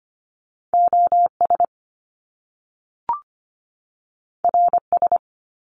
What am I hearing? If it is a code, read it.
Morse code, heard as a keyed tone, about 25 wpm, 700 Hz. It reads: OS RS
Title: Top Two Letter Combinations in Words 25wpm